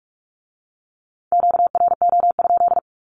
Morse code, heard as a keyed tone, about 46 wpm, 700 Hz. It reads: QRO?
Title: Q-Codes with Spoken Meaning - 45wpm